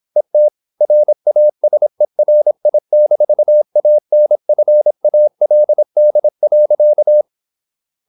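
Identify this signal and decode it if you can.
Morse code, heard as a keyed tone, about 26 wpm, 600 Hz. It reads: ET RASERI-ANFALD.
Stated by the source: High-speed (QRQ) Morse at 26 wpm.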